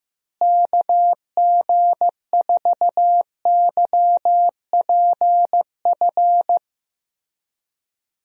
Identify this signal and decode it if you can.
Morse code, heard as a keyed tone, about 16 wpm, 700 Hz. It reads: KG4YPF
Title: USA Call Signs - 6 Characters - Repeated 3x - 15wpm